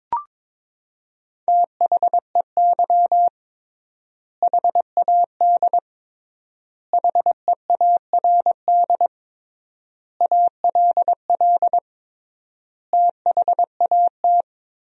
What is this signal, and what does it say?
Morse code, heard as a keyed tone, about 22 wpm, 700 Hz. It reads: THEY HAD HEARD ALL THAT
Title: 3x Word-Spacing - Sentences from Top 400 Words - 22wpm